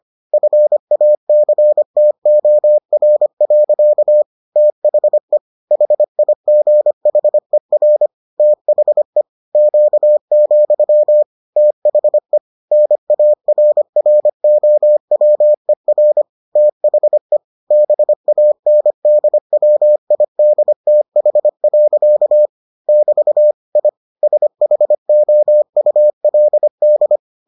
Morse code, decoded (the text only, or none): FACTOR. THE HIGHER THE Q, THE NARROWER THE BANDWIDTH. <BT> I SHOULD